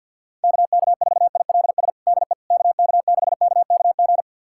Morse code, heard as a keyed tone, about 50 wpm, 700 Hz. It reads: KK4ILS DE KK6KKG
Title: USA Callsign DE Callsign - 6 Characters - 50wpm